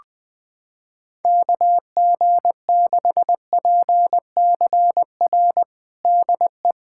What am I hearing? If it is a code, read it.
Morse code, heard as a keyed tone, about 21 wpm, 700 Hz. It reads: KG6PCR DE
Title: USA Callsign DE Callsign - 6 Characters - 20wpm